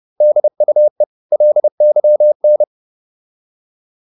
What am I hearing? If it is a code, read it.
Morse code, heard as a keyed tone, about 30 wpm, 600 Hz. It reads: DUE LYN
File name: Korte ord DK 30wpm